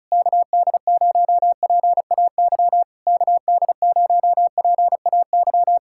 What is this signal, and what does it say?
Morse code, heard as a keyed tone, about 37 wpm, 700 Hz. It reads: KD0PAY KD0PAY
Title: Speed-Racing Common QSO Phrases - 30wpm